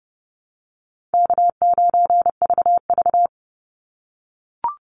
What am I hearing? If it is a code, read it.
Morse code, heard as a keyed tone, about 30 wpm, 700 Hz. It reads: K9VV
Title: Call Signs - 30wpm